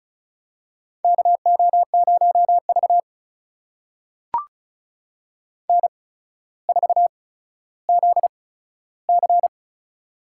Morse code, heard as a keyed tone, about 36 wpm, spaced as for 15 wpm, 700 Hz. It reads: KO0V N4ZC
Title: USA Call Signs - 4 Characters - 8x Character Spacing - 35wpm